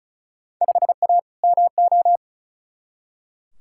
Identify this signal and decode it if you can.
Morse code, heard as a keyed tone, about 36 wpm, 700 Hz. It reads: 5A MO
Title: Field Day Exchanges Encore - 35wpm